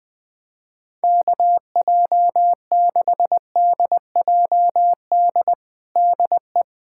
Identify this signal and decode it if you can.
Morse code, heard as a keyed tone, about 20 wpm, 700 Hz. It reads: KJ6DJD DE
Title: USA Callsign DE Callsign 20wpm